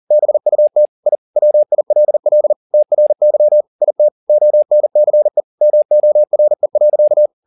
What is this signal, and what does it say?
Morse code, heard as a keyed tone, about 40 wpm, 600 Hz. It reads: BUT I WILL TRY IT ONCE MORE.
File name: the old man and the sea 2 40wpm